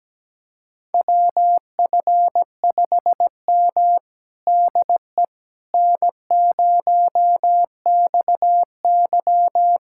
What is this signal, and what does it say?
Morse code, heard as a keyed tone, about 17 wpm, 700 Hz. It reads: WF5M DE N0XY
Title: USA Callsign DE Callsign - 4 Characters - 17wpm